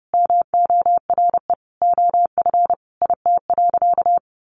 Morse code, heard as a keyed tone, about 30 wpm, 700 Hz. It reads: MORE OF IT.
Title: Sentences from Top 1300 Words 30wpm